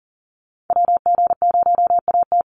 Morse code, heard as a keyed tone, about 40 wpm, 700 Hz. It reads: WG0AT
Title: NA SOTA Activator Callsigns 40wpm